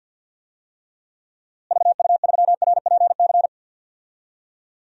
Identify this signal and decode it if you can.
Morse code, heard as a keyed tone, about 46 wpm, 700 Hz. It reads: VU3RPC
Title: DXCC-100 Call Signs - 6 Characters - Repeated 2x - 50wpm